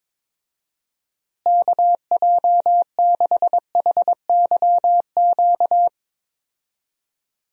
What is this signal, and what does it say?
Morse code, heard as a keyed tone, about 22 wpm, 700 Hz. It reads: KJ6HYQ
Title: USA Call Signs - 6 Characters - Repeated 2x - 22wpm